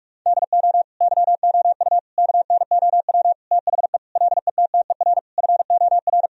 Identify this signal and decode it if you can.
Morse code, heard as a keyed tone, about 45 wpm, 700 Hz. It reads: DO YOU KNOW THE LETTER FOR
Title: Sentences from Top 1000 Words - Review Part 2 - 45wpm